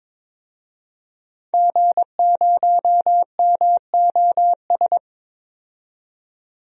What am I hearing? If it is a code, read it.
Morse code, heard as a keyed tone, about 21 wpm, 700 Hz. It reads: G0MOS
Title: DXCC-100 Call Signs - 5 Characters - 22wpm